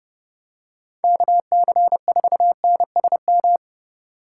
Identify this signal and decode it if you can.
Morse code, heard as a keyed tone, about 30 wpm, 700 Hz. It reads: KC4NSM
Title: USA Call Signs - 6 Characters - 2x Character Spacing - 30wpm